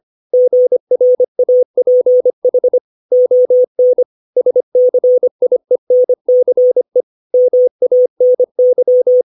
Morse code, heard as a keyed tone, about 25 wpm, 500 Hz. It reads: GRAPH ON SCIENCE MANY
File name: 632 words / 632 Words 25WPM